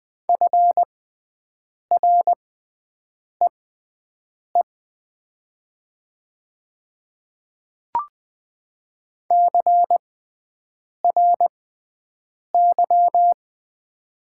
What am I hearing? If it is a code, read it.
Morse code, heard as a keyed tone, about 20 wpm, spaced as for 7 wpm, 700 Hz. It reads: FREE CRY
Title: 6x Character Spacing - Top 500 Words - 20wpm